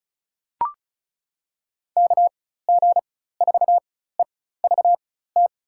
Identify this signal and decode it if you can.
Morse code, heard as a keyed tone, about 34 wpm, spaced as for 16 wpm, 700 Hz. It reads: KG4EVT
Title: USA Call Signs - 6 Characters - 4x Character Spacing - 35wpm